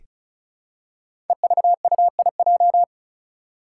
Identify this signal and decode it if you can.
Morse code, heard as a keyed tone, about 36 wpm, 700 Hz. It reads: EVUIJ
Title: Five Challenging Letter-Number Combinations 35wpm